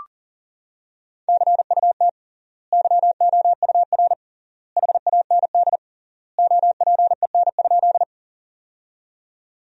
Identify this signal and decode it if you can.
Morse code, heard as a keyed tone, about 40 wpm, 700 Hz. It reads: CUT YOUR HAND OPEN?
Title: 3x Word-Spacing - Sentences from Top 300 Words - 40wpm